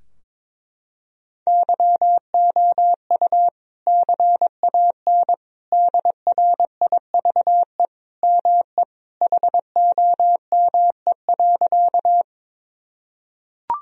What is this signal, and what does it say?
Morse code, heard as a keyed tone, about 22 wpm, 700 Hz. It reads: YOU CAN DRIVE ME HOME.
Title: Sentences from Top 1000 Words - Review Part 2 - 22wpm